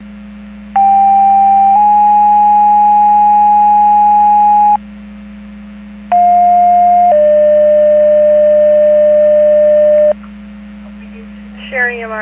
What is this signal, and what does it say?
Two-tone page, heard as a one-second tone, then a three-second tone, about 790 then 830 Hz, about 710 then 600 Hz. It is heard from 0.8 to 10.1 s.
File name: Portage 154.385 Paging